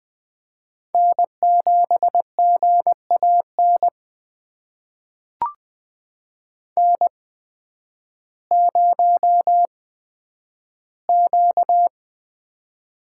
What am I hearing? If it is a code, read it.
Morse code, heard as a keyed tone, about 20 wpm, spaced as for 11 wpm, 700 Hz. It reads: N7GAN N0Q
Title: USA Call Signs - 5 Characters - 8x Character Spacing - 20wpm